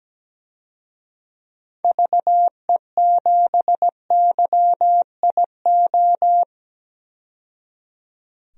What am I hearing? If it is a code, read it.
Morse code, heard as a keyed tone, about 17 wpm, 700 Hz. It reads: VE7YIO
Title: DXCC-100 Call Signs - 6 Characters - 17wpm